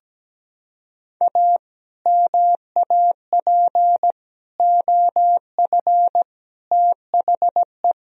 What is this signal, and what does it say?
Morse code, heard as a keyed tone, about 17 wpm, 700 Hz. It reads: A MAP OF THE
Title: Sentences from Top 1000 Words - Review Part 2 - 17wpm